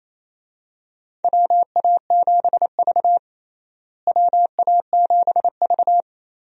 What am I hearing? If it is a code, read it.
Morse code, heard as a keyed tone, about 28 wpm, 700 Hz. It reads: WA7V WA7V
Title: USA Call Signs - 4 Characters - Repeated 2x - 28wpm